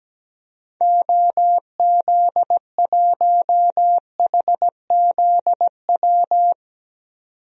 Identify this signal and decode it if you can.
Morse code, heard as a keyed tone, about 17 wpm, 700 Hz. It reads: OZ1HZW
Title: DXCC-100 Call Signs - 6 Characters - 17wpm